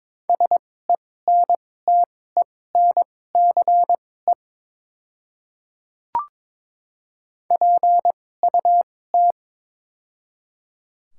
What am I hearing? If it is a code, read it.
Morse code, heard as a keyed tone, about 22 wpm, spaced as for 16 wpm, 700 Hz. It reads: SENTENCE PUT
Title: 2x Character Spacing - Top 300 Words - 22wpm